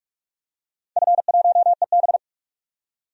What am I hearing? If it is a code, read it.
Morse code, heard as a keyed tone, about 45 wpm, 700 Hz. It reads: F1EB
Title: DXCC-100 Call Signs - 4 Characters - 45wpm